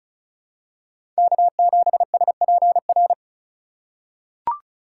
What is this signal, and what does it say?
Morse code, heard as a keyed tone, about 34 wpm, 700 Hz. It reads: K7SPR
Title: USA Call Signs - 5 Characters - 3x Character Spacing - 35wpm